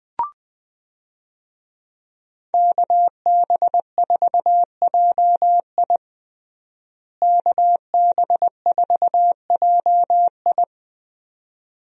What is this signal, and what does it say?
Morse code, heard as a keyed tone, about 20 wpm, 700 Hz. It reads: KB4JI KB4JI
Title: USA Call Signs - 5 Characters - Repeated 2x - 20wpm